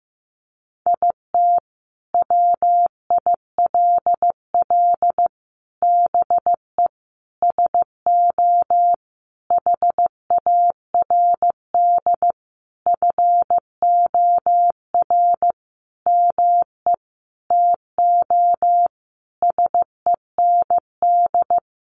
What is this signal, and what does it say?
Morse code, heard as a keyed tone, about 15 wpm, 700 Hz. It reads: IT WILL BE SO HARD FOR ME TO SEND
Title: Sentences from Top 1500 Words 15wpm